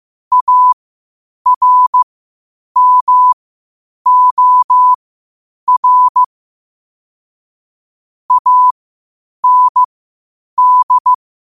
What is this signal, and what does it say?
Morse code, heard as a keyed tone, about 15 wpm, spaced as for 8 wpm, 1000 Hz. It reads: ARMOR AND
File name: qotdfiles / 8wpm